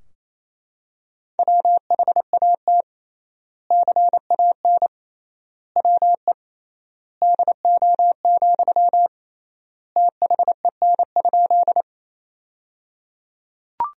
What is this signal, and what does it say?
Morse code, heard as a keyed tone, about 27 wpm, 700 Hz. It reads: WHAT CAN WE DO, THEN?
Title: 3x Word-Spacing - Sentences from Top 100 Words - 28wpm